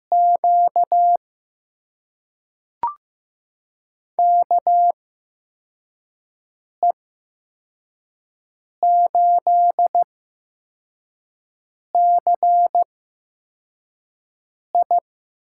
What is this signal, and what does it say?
Morse code, heard as a keyed tone, about 15 wpm, spaced as for 4 wpm, 700 Hz. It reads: Q KE8CI
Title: USA Call Signs - 6 Characters - 8x Character Spacing - 15wpm